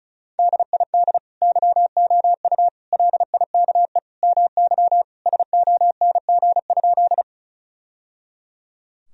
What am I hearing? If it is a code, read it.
Morse code, heard as a keyed tone, about 36 wpm, 700 Hz. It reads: DID YOU LIKE MY SONG?
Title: Sentences from Top 1000 Words - Review Part 2 - 35wpm